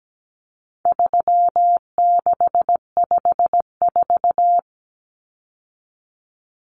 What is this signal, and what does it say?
Morse code, heard as a keyed tone, about 17 wpm, 700 Hz. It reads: 3654
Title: Common Cut Numbers - 4-Digit 17wpm